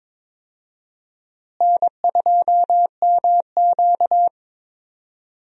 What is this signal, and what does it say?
Morse code, heard as a keyed tone, about 22 wpm, 700 Hz. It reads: N2MQ
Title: USA Call Signs - 4 Characters - Repeated 3x - 22wpm